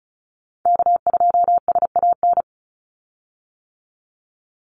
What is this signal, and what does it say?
Morse code, heard as a keyed tone, about 34 wpm, 700 Hz. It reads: K2SAN
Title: Call Signs - 35wpm